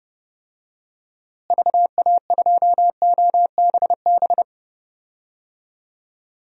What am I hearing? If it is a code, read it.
Morse code, heard as a keyed tone, about 30 wpm, 700 Hz. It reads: VA2OBB
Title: DXCC-100 Call Signs - 6 Characters - Repeated 3x - 30wpm